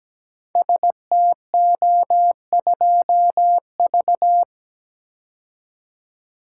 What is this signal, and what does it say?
Morse code, heard as a keyed tone, about 17 wpm, 700 Hz. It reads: STO2V
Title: Code Groups - 5 Letter:Numbers - 17wpm